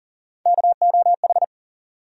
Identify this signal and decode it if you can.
Morse code, heard as a keyed tone, about 40 wpm, 700 Hz. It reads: KOH
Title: USA Call Sign Postfixes - 3 Letters 40wpm